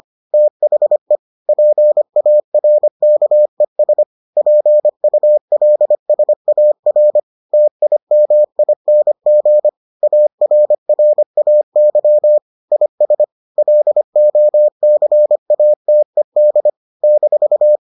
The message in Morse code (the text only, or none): THE PARKES PULSAR TIMING ARRAY IS LOCATED -